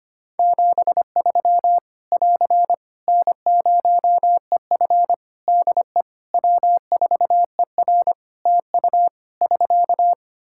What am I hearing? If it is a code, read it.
Morse code, heard as a keyed tone, about 25 wpm, 700 Hz. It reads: 73 <AR> N0EF DE W4ER TU <SK>
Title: Practice QSO - Exchange 4 - Ending - 25wpm